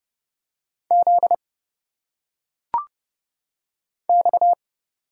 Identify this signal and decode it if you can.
Morse code, heard as a keyed tone, about 30 wpm, 700 Hz. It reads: Z X
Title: Single Letters - Phonetic Pronunciation 30wpm